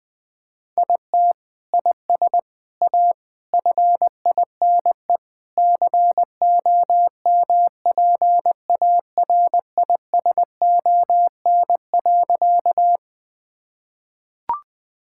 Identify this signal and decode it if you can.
Morse code, heard as a keyed tone, about 20 wpm, 700 Hz. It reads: IT IS A FINE COMPARISON.
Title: Sentences from Top 1600 Words 20wpm